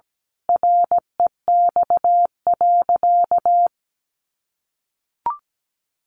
Morse code, heard as a keyed tone, about 17 wpm, 700 Hz. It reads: REX.
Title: A Sound Of Thunder 17wpm